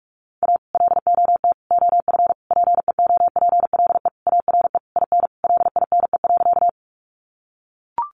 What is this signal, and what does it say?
Morse code, heard as a keyed tone, about 44 wpm, 700 Hz. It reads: A LOT OF PEOPLE ARE IN LINE.
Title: Additional Easy Phrases 45wpm